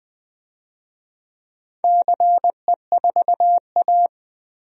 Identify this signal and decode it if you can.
Morse code, heard as a keyed tone, about 20 wpm, 700 Hz. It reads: CE4A
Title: DXCC-100 Call Signs - 4 Characters - 20wpm